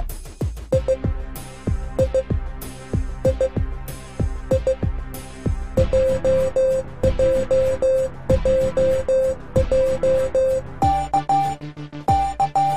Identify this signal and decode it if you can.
Morse code, heard as a keyed tone, about 16 wpm, 520 Hz. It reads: I I I I JJJJ
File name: cwa / MorseSong-A-Z